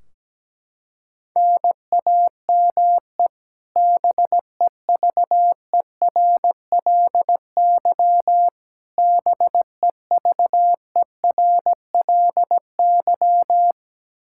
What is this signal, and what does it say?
Morse code, heard as a keyed tone, about 17 wpm, 700 Hz. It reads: NAME BEVERLY BEVERLY
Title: Practice QSO - Exchange 1 - Beginning - 17wpm